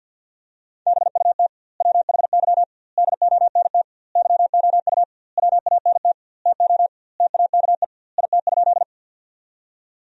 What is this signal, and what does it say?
Morse code, heard as a keyed tone, about 50 wpm, 700 Hz. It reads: BUT WHY DONT YOU WANT TO TAKE IT?
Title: 2x Word-Spacing - Sentences from Top 300 Words - 50wpm